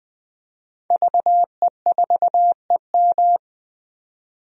Morse code, heard as a keyed tone, about 20 wpm, 700 Hz. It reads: VE4EM
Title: DXCC-100 Call Signs - 5 Characters - 20wpm